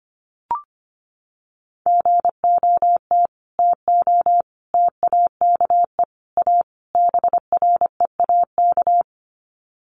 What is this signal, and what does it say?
Morse code, heard as a keyed tone, about 25 wpm, 700 Hz. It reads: GOT TO TAKE A BREAK
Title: Sets of 5 Words Encore 25wpm